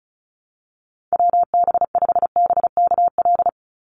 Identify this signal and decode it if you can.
Morse code, heard as a keyed tone, about 34 wpm, 700 Hz. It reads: WB5BKL
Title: NA SOTA Chaser Callsigns 35wpm